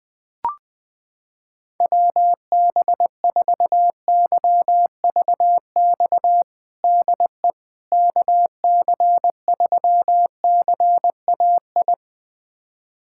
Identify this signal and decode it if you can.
Morse code, heard as a keyed tone, about 20 wpm, 700 Hz. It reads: WB4YVX DE KC3CAI
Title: USA Callsign DE Callsign - 6 Characters - 20wpm